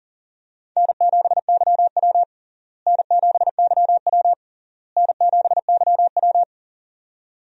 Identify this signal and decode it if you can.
Morse code, heard as a keyed tone, about 40 wpm, 700 Hz. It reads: N7YW N7YW N7YW
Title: USA Call Signs - 4 Characters - Repeated 3x - 40wpm